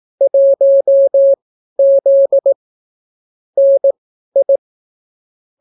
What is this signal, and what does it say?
Morse code, heard as a keyed tone, about 18 wpm, spaced as for 12 wpm, 550 Hz. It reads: 1Z NI